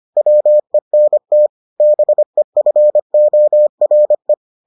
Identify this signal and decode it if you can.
Morse code, heard as a keyed tone, about 25 wpm, 600 Hz. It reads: WENT BEFORE